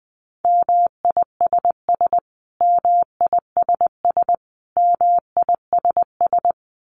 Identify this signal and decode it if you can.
Morse code, heard as a keyed tone, about 20 wpm, 700 Hz. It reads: MISS MISS MISS
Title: Top 500 Words Repeated 20wpm